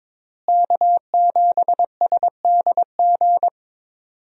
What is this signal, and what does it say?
Morse code, heard as a keyed tone, about 22 wpm, 700 Hz. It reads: K7SDG
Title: USA Call Signs - 5 Characters - Repeated 3x - 22wpm